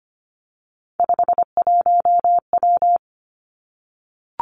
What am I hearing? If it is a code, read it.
Morse code, heard as a keyed tone, about 25 wpm, 700 Hz. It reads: 51W
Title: Three Challenging Letter-Number Combinations 25wpm